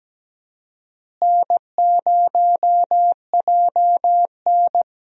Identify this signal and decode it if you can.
Morse code, heard as a keyed tone, about 17 wpm, 700 Hz. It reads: N0JN
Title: USA Call Signs - 4 Characters - Repeated 2x - 17wpm